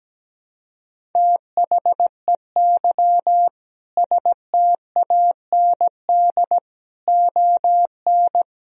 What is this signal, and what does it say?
Morse code, heard as a keyed tone, about 17 wpm, 700 Hz. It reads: THEY STAND ON
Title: Sentences from Top 1000 Words - Review Part 1 - 17wpm